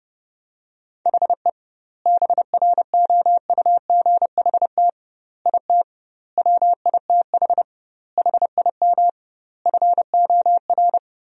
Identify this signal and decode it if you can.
Morse code, heard as a keyed tone, about 31 wpm, 700 Hz. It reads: HE BROUGHT IT WITH HIM FOR
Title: 2x Word-Spacing - Sentences from Top 500 Words - 30wpm